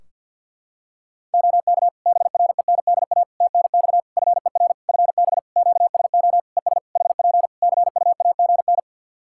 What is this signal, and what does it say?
Morse code, heard as a keyed tone, about 51 wpm, 700 Hz. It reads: OK BRENDA TNX FER FB QSO ES HP CUAGN
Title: Practice QSO - Exchange 4 - Ending - 50wpm